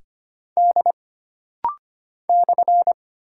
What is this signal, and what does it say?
Morse code, heard as a keyed tone, about 25 wpm, 700 Hz. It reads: D /
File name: Single Character (Letter:Number:Punctuation:Prosign) - Rapid-Fire - 25wpm